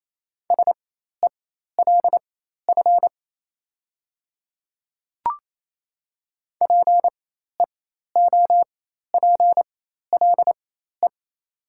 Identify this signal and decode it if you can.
Morse code, heard as a keyed tone, about 28 wpm, spaced as for 13 wpm, 700 Hz. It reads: SELF PEOPLE
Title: 4x Character Spacing - Top 500 Words - 28wpm